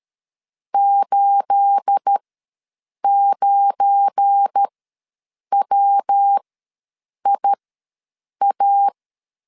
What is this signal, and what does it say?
Morse code, heard as a keyed tone, about 13 wpm, spaced as for 7 wpm, 800 Hz. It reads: 89WIA
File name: LEZIONI / 9 28 ws 12wpm 3354